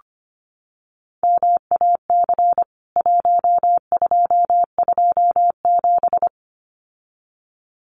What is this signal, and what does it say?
Morse code, heard as a keyed tone, about 25 wpm, 700 Hz. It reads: MAC 1227
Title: CWT Exchange 25wpm